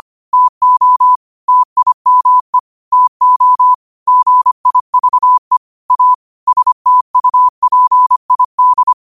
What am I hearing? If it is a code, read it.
Morse code, heard as a keyed tone, about 25 wpm, 1000 Hz. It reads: TO TIME TO GIVE A STUPID